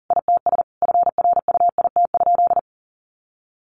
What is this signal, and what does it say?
Morse code, heard as a keyed tone, about 40 wpm, 700 Hz. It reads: ITS FRUIT?
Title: Sentences from Top 1500 Words 40wpm